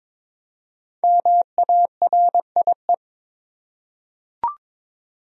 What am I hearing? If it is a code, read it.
Morse code, heard as a keyed tone, about 22 wpm, 700 Hz. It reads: MARIE
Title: USA Common Names - 22wpm